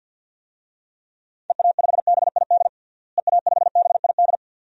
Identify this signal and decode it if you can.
Morse code, heard as a keyed tone, about 47 wpm, 700 Hz. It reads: EA5BID EA5BID
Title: DXCC-100 Call Signs - 6 Characters - Repeated 2x - 50wpm